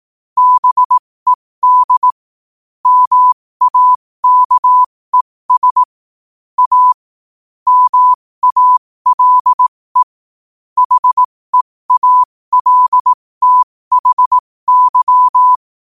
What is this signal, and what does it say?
Morse code, heard as a keyed tone, about 18 wpm, spaced as for 15 wpm, 1000 Hz. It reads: BED MAKES A MALE HEALTHY
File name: qotdfiles / 15wpm